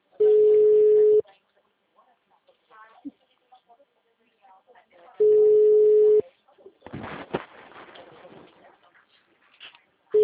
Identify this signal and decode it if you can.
Ringback tone (Europe, 425 Hz), with speech.